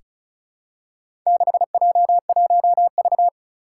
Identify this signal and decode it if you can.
Morse code, heard as a keyed tone, about 36 wpm, 700 Hz. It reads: 6J1V
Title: Four Challenging Letter-Number Combinations 35wpm